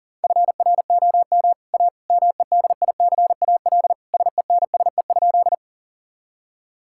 Morse code, heard as a keyed tone, about 40 wpm, 700 Hz. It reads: FROM A MEDICAL SENSE?
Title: Sentences from Top 1000 Words - Review Part 5 - 40wpm